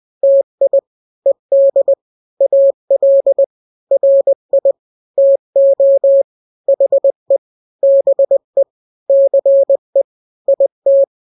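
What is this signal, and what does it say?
Morse code, heard as a keyed tone, about 20 wpm, 550 Hz. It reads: TI ED AL RI TO HE BE CE IT
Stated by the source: Morse drill of two-letter n-grams at 20/18 WPM (Farnsworth), from KK7RR.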